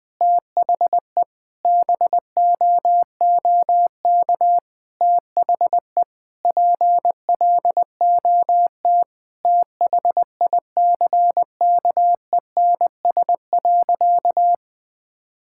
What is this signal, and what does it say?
Morse code, heard as a keyed tone, about 21 wpm, 700 Hz. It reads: THE BOOK THE PLOT THICKENS.
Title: Sentences with 5-Word Phrases - Spoken Beginning - 20wpm